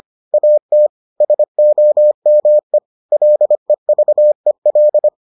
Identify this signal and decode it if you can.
Morse code, heard as a keyed tone, about 25 wpm, 600 Hz. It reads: AT SOME LEVEL